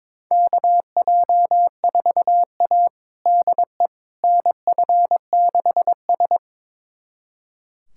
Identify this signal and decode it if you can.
Morse code, heard as a keyed tone, about 22 wpm, 700 Hz. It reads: KJ4A DE NF6S
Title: USA Callsign DE Callsign - 4 Characters - 22wpm